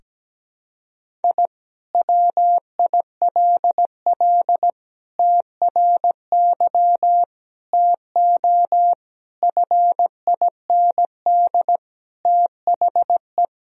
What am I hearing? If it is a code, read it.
Morse code, heard as a keyed tone, about 17 wpm, 700 Hz. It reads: I WILL TRY TO FIND THE
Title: Sentences from Top 1000 Words - Review Part 4 - 17wpm